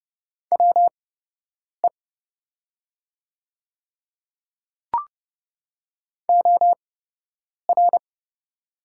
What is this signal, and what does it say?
Morse code, heard as a keyed tone, about 31 wpm, spaced as for 8 wpm, 700 Hz. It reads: WE OR
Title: 8x Character Spacing - Top 100 Words - 30wpm